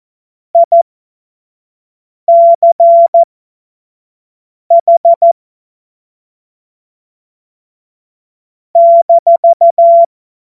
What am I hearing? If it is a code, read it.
Morse code, heard as a keyed tone, about 14 wpm, spaced as for 5 wpm, 680 Hz. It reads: ICH -
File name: Morsekurs / Witze / Witz 10. August 2021, 25 F70